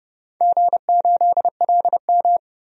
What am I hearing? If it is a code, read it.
Morse code, heard as a keyed tone, about 31 wpm, 700 Hz. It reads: G8LM
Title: DXCC-100 Call Signs - 4 Characters - 30wpm